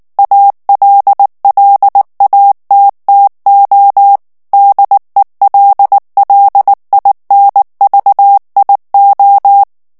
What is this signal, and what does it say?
Morse code, heard as a keyed tone, about 19 wpm, 800 Hz. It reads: ALLATTO DELLINVIO
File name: morse mp3 / ITALIEN texte clair 2 vitesse 18 mots minute